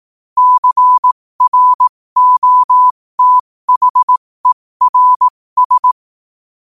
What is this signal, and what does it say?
Morse code, heard as a keyed tone, about 18 wpm, spaced as for 15 wpm, 1000 Hz. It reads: CROTHERS